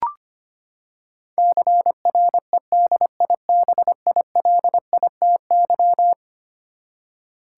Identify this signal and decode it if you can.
Morse code, heard as a keyed tone, about 25 wpm, 700 Hz. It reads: CREDIBILITY
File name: practice / 25wpm / Eleven Letter Words - 25wpm